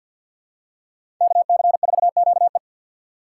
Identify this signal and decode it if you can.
Morse code, heard as a keyed tone, about 52 wpm, 700 Hz. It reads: KC4QE
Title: USA Call Signs - 5 Characters - 6x Character Spacing - 50wpm